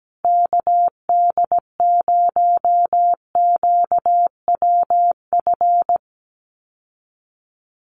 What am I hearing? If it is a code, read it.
Morse code, heard as a keyed tone, about 17 wpm, 700 Hz. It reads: KD0QWF
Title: USA Call Signs Repeated 3x 17wpm